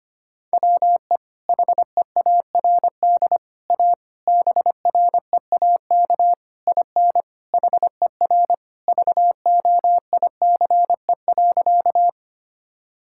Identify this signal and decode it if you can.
Morse code, heard as a keyed tone, about 25 wpm, 700 Hz. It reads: WE HEARD A BREAK IN HER VOICE.
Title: Sentences from Top 1000 Words - Review Part 6 - 25wpm